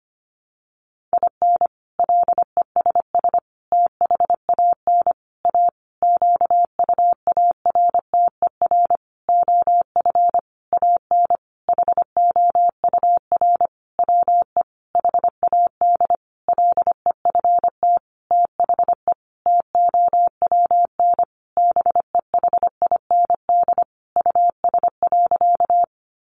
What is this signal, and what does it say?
Morse code, heard as a keyed tone, about 25 wpm, 700 Hz. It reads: IN LESS THAN A QUARTER OF AN HOUR WE HAD LEFT THE TOWN BEHIND US.
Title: Sentences from Top 1300 Words 25wpm